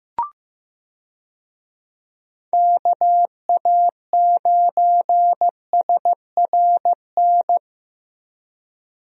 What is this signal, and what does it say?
Morse code, heard as a keyed tone, about 15 wpm, 700 Hz. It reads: KA9SRN
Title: USA Call Signs - 6 Characters - 15wpm